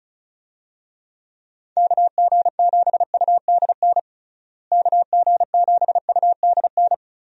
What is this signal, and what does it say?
Morse code, heard as a keyed tone, about 34 wpm, 700 Hz. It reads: KG7UDN KG7UDN
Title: USA Call Signs - 6 Characters - Repeated 3x - 35wpm